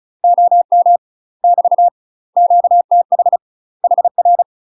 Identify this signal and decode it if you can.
Morse code, heard as a keyed tone, about 34 wpm, 700 Hz. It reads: OM <BT> QTH HR
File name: UK MASTER QSOs TRAINING / 35WPM / UK MASTER QSOs 0012